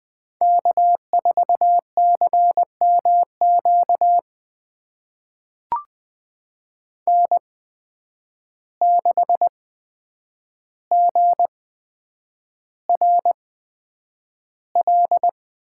Morse code, heard as a keyed tone, about 20 wpm, spaced as for 9 wpm, 700 Hz. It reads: K4CMQ N6GRL
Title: USA Call Signs - 8x Character Spacing - 20wpm